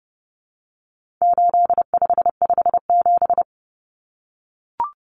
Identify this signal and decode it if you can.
Morse code, heard as a keyed tone, about 30 wpm, 700 Hz. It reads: 8557
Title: Numbers - 4-Digit 30wpm